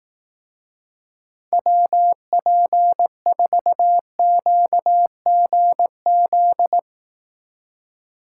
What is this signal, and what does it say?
Morse code, heard as a keyed tone, about 18 wpm, 700 Hz. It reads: WP4QGZ
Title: Speed Racing - USA Call Signs 15wpm